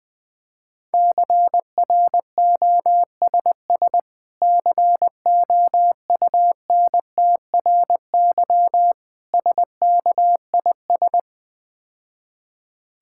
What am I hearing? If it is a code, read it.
Morse code, heard as a keyed tone, about 20 wpm, 700 Hz. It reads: CROSS COUNTRY SKIS
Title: Difficult Words - Speed-Racing with Hints - 20wpm